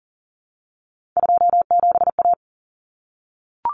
Morse code, heard as a keyed tone, about 40 wpm, 700 Hz. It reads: 27A
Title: Common Cut Numbers - 3-Digit 40wpm